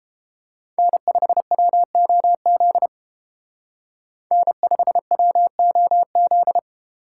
Morse code, heard as a keyed tone, about 30 wpm, 700 Hz. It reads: N5WOZ N5WOZ